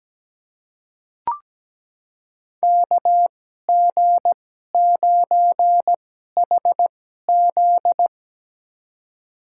Morse code, heard as a keyed tone, about 17 wpm, spaced as for 12 wpm, 700 Hz. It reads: KG9HZ
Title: USA Call Signs - 5 Characters - 2x Character Spacing - 17wpm